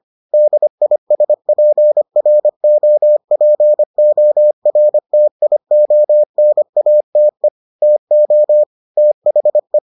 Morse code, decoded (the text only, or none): DISPROPORTIONATE TO THE